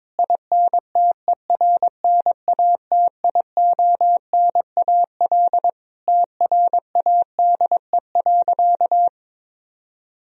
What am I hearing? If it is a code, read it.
Morse code, heard as a keyed tone, about 22 wpm, 700 Hz. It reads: INTERNATIONAL TRADE.
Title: Sentences from Top 1000 Words - Review Part 4 - 22wpm